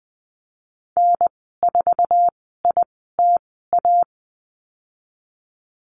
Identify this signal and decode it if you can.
Morse code, heard as a keyed tone, about 20 wpm, spaced as for 15 wpm, 700 Hz. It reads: N4ITA
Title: 055 Lesson 10g - Letter r - Call Signs 20wpm